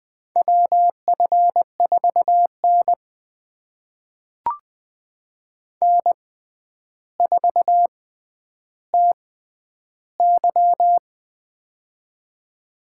Morse code, heard as a keyed tone, about 21 wpm, spaced as for 10 wpm, 700 Hz. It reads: WF4N N4TY
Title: USA Call Signs - 4 Characters - 6x Character Spacing - 20wpm